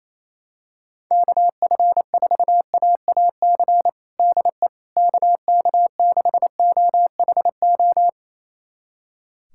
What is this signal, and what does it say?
Morse code, heard as a keyed tone, about 28 wpm, 700 Hz. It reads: KF4AAC DE KK6OHO
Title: USA Callsign DE Callsign 28wpm